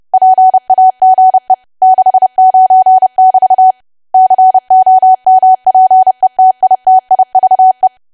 Morse code, heard as a keyed tone, about 30 wpm, 750 Hz. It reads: PAGE 69<BT> COMPETITIVE
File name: cwa / W1AW-Level3 / 30WPM